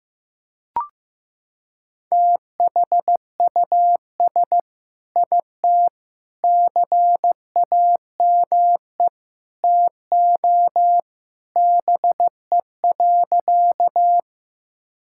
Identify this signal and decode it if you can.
Morse code, heard as a keyed tone, about 15 wpm, 690 Hz. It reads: THUS IT CAME TO BE.
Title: Sentences from Top 1000 Words - Review Part 5 - 15wpm